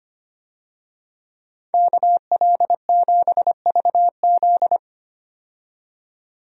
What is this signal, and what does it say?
Morse code, heard as a keyed tone, about 25 wpm, 700 Hz. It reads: KL7VZ
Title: DXCC-100 Call Signs - 5 Characters - 25wpm